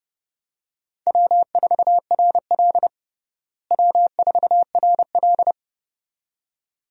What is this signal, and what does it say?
Morse code, heard as a keyed tone, about 32 wpm, 700 Hz. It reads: W4RL W4RL
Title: USA Call Signs - 4 Characters - Repeated 2x - 30wpm